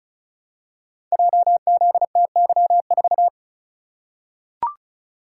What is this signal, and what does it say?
Morse code, heard as a keyed tone, about 34 wpm, 700 Hz. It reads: JZTY4
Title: Code Groups - 5 Letter:Numbers - 35wpm